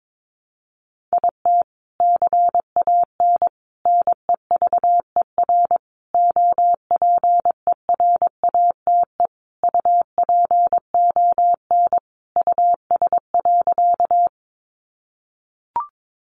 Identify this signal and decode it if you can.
Morse code, heard as a keyed tone, about 22 wpm, 700 Hz. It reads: IT CAN NEVER OPERATE UPON US.
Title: Sentences from Top 1500 Words 22wpm